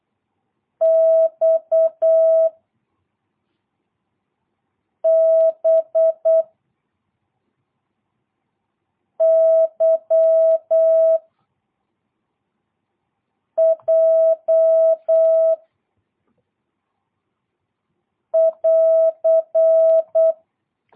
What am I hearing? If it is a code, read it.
Morse code, heard as a keyed tone, about 8 wpm, 660 Hz. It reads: X B Y J <AR>
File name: jh8cbh / 2022cw